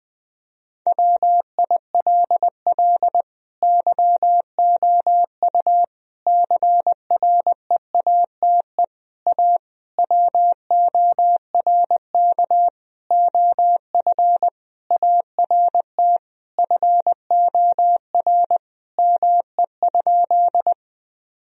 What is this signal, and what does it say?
Morse code, heard as a keyed tone, about 21 wpm, 700 Hz. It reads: WILL YOU CREATE A WORK OF ART FOR ME?
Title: Sentences from Top 1000 Words - Review Part 3 - 20wpm